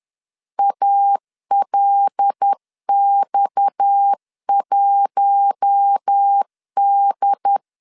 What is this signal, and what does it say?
Morse code, heard as a keyed tone, about 10 wpm, 800 Hz. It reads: ALX1D
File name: LEZIONI / 6 PKX0 / 6 PKX 10wpm 2426